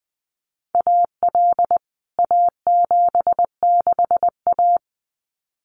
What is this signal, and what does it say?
Morse code, heard as a keyed tone, about 20 wpm, 700 Hz. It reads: AL A76A
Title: CWT Exchange 20wpm